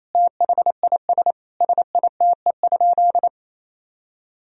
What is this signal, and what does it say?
Morse code, heard as a keyed tone, about 28 wpm, 700 Hz. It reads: THIS SITE?
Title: Sentences from Top 1000 Words - Review Part 4 - 28wpm